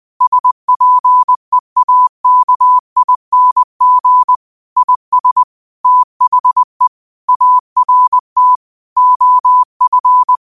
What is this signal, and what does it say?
Morse code, heard as a keyed tone, about 20 wpm, 1000 Hz. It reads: SPEAKING IS THE ART OF
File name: qotdfiles / 20wpm